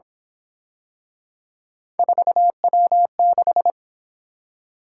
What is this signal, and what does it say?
Morse code, heard as a keyed tone, about 27 wpm, 700 Hz. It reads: 4W6
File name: Speed-Racing - Three Letter-Number Combinations - 22wpm